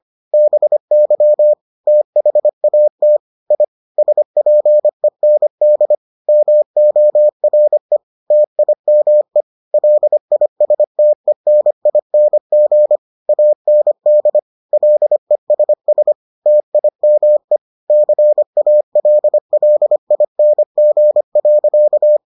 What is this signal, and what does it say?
Morse code, heard as a keyed tone, about 25 wpm, 600 Hz. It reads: BY THAT I SPEND MORE TIME LISTENING AND LESS TIME CALLING.